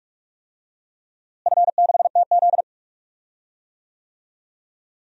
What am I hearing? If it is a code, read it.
Morse code, heard as a keyed tone, about 46 wpm, 700 Hz. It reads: F6TZ
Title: DXCC-100 Call Signs - 4 Characters - Repeated 3x - 45wpm